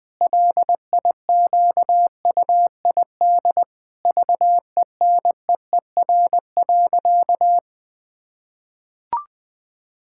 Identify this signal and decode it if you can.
Morse code, heard as a keyed tone, about 20 wpm, 700 Hz. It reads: LIQUID VENEER.
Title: Pangram Challenge v2 - 20wpm